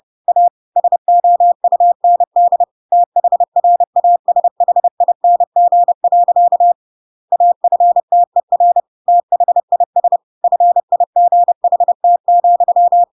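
Morse code, heard as a keyed tone, about 30 wpm, 700 Hz. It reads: A SOUND THRASHING. AFTER THIS FIGHT,
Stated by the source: Chapter 2 of The Adventures of Pinocchio read as a book in Morse code (CW) at 30 wpm, by ik1hge.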